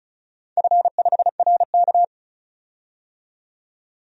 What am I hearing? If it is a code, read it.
Morse code, heard as a keyed tone, about 36 wpm, 690 Hz. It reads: F5RK
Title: DXCC-100 Call Signs - 4 Characters - 35wpm